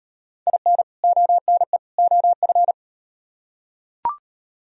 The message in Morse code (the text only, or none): IN ONE OF